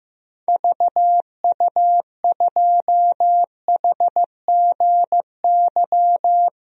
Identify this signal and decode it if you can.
Morse code, heard as a keyed tone, about 15 wpm, 700 Hz. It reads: VU2HGY
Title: DXCC-100 Call Signs - 6 Characters - Repeated 3x - 15wpm